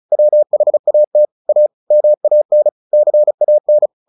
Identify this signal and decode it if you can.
Morse code, heard as a keyed tone, about 34 wpm, 600 Hz. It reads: WHAT A MAN CAN
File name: the old man and the sea / the old man and the sea 1 35wpm